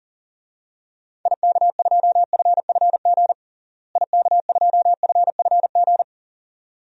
Morse code, heard as a keyed tone, about 37 wpm, 700 Hz. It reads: IK2FFG IK2FFG
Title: DXCC-100 Call Signs - 6 Characters - Repeated 2x - 40wpm